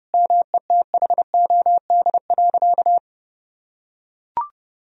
Morse code, heard as a keyed tone, about 32 wpm, 700 Hz. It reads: METHOD.
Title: Sentences from Top 1000 Words - Review Part 4 - 30wpm